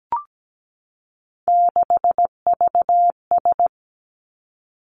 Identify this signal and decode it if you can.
Morse code, heard as a keyed tone, about 17 wpm, 700 Hz. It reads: 6VS
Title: Three Challenging Letter-Number Combinations 17wpm